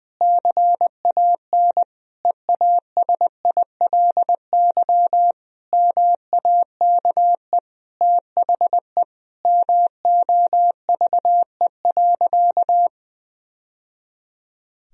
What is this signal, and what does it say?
Morse code, heard as a keyed tone, about 20 wpm, 700 Hz. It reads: CAN EASILY MAKE THE MOVE.
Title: Sentences from Top 1000 Words - Review Part 1 - 20wpm